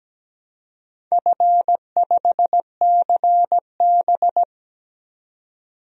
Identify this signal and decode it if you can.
Morse code, heard as a keyed tone, about 17 wpm, 700 Hz. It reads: F5CB
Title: DXCC-100 Call Signs - 4 Characters - 17wpm